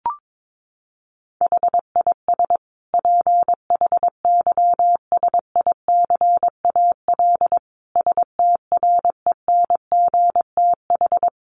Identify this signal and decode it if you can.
Morse code, heard as a keyed tone, about 22 wpm, 700 Hz. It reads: HIS PHYSICAL STRENGTH
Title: Sentences from Top 1400 Words 22wpm